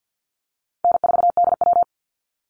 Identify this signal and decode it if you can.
Morse code, heard as a keyed tone, about 50 wpm, 700 Hz. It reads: N4DW
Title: Call Signs - 50wpm